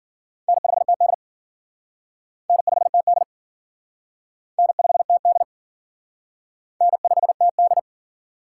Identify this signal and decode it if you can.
Morse code, heard as a keyed tone, about 52 wpm, 700 Hz. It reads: N5TD N5TD N5TD N5TD